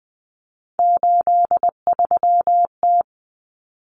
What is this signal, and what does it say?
Morse code, heard as a keyed tone, about 20 wpm, 700 Hz. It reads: 83T
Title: Common Cut Numbers - 3-Digit 20wpm